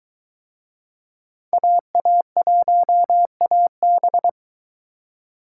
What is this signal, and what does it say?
Morse code, heard as a keyed tone, about 23 wpm, 700 Hz. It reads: AA1AB